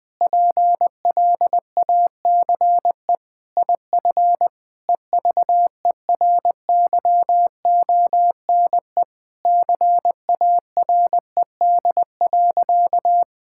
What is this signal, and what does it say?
Morse code, heard as a keyed tone, about 20 wpm, 700 Hz. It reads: PLACE IF EVERYONE CARED.
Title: Sentences with 8-Word Phrases - Spoken Beginning - 20wpm